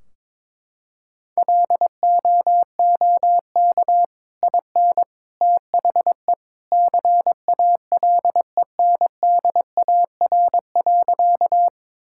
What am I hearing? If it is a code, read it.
Morse code, heard as a keyed tone, about 22 wpm, 700 Hz. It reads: LOOK IN THE CALENDAR.
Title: Sentences from Top 1600 Words 22wpm